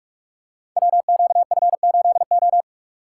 Morse code, heard as a keyed tone, about 44 wpm, 700 Hz. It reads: WQF8O
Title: Code Groups - 5 Letter:Numbers - 45wpm